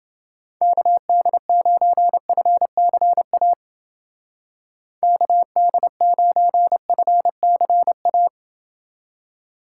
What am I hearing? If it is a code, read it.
Morse code, heard as a keyed tone, about 29 wpm, 700 Hz. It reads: KD9FCA KD9FCA